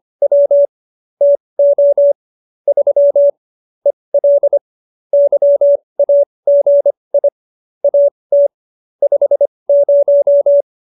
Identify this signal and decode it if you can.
Morse code, heard as a keyed tone, about 25 wpm, spaced as for 20 wpm, 570 Hz. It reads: W TO 3 EL YAGI AT 50
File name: QSO301 20